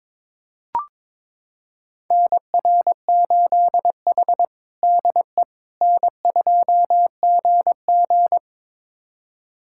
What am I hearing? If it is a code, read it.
Morse code, heard as a keyed tone, about 22 wpm, 700 Hz. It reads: NR8H DE N2GG
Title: USA Callsign DE Callsign - 4 Characters - 22wpm